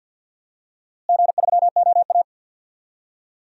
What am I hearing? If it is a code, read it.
Morse code, heard as a keyed tone, about 50 wpm, 700 Hz. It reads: G3OA
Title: DXCC-100 Call Signs - 4 Characters - Repeated 2x - 50wpm